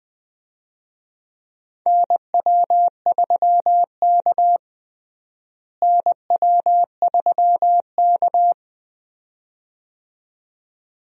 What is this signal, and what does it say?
Morse code, heard as a keyed tone, about 20 wpm, 700 Hz. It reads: NW3K NW3K
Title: USA Call Signs - 4 Characters - Repeated 2x - 20wpm